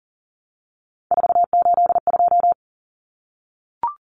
Morse code, heard as a keyed tone, about 40 wpm, 700 Hz. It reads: 482
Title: Common Cut Numbers - 3-Digit 40wpm